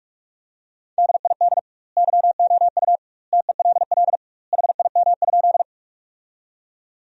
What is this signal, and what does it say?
Morse code, heard as a keyed tone, about 46 wpm, 700 Hz. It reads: DID YOU TELL HIM?
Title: 2x Word-Spacing - Sentences from Top 200 Words - 45wpm